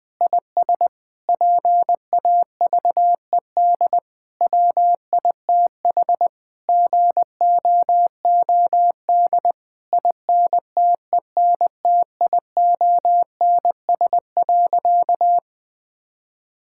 Morse code, heard as a keyed tone, about 19 wpm, 700 Hz. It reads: IS PAVED WITH GOOD INTENTIONS.
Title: Sentences with 8-Word Phrases - 20wpm